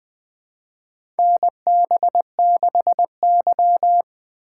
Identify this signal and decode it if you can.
Morse code, heard as a keyed tone, about 20 wpm, 700 Hz. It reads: NB6Y
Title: USA Call Signs - 6x Character Spacing - 20wpm